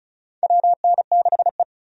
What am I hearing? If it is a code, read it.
Morse code, heard as a keyed tone, about 36 wpm, 700 Hz. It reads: WN6E
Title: USA Call Signs - 4 Characters - 35wpm